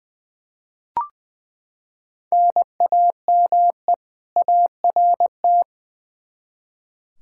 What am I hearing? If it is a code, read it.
Morse code, heard as a keyed tone, about 20 wpm, 700 Hz. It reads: NAME ART
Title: K1USN SST QSO Practice 20wpm